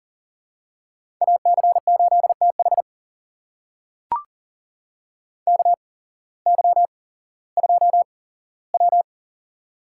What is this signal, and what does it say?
Morse code, heard as a keyed tone, about 40 wpm, spaced as for 19 wpm, 700 Hz. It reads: AC8TH KY2W
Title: USA Call Signs - 8x Character Spacing - 40wpm